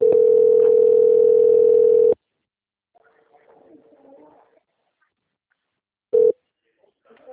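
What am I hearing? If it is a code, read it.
Ringback tone (North America), 440 + 480 Hz.
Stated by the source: Outgoing telephone call recorded through a system in Cornelio Procopio - PR.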